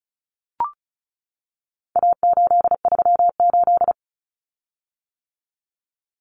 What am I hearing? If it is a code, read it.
Morse code, heard as a keyed tone, about 34 wpm, 700 Hz. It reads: A838
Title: Common Cut Numbers - 4-Digit 35wpm